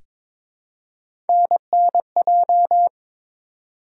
Morse code, heard as a keyed tone, about 22 wpm, 700 Hz. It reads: NNJ
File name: ARRL Sections v2 - 22wpm